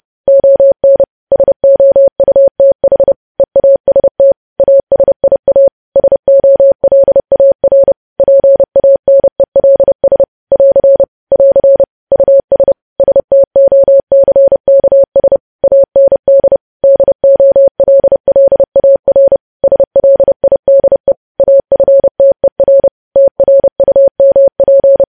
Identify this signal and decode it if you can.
Morse code, heard as a keyed tone, about 30 wpm, 550 Hz. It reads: ON SOUTH EAST ASIA SOLAR PANELS <AR> <AR> US STOCKS AND DOLLAR SLIDE AFTER TRUMP
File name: podcast / News headlines-1x-30x30